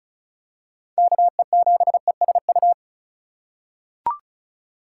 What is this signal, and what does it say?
Morse code, heard as a keyed tone, about 36 wpm, 700 Hz. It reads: KE7ESU
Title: USA Call Signs - 6 Characters - 3x Character Spacing - 35wpm